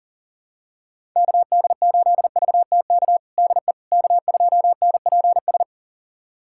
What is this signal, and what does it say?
Morse code, heard as a keyed tone, about 37 wpm, 700 Hz. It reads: KD8VTK DE K2NPS
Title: USA Callsign DE Callsign 40wpm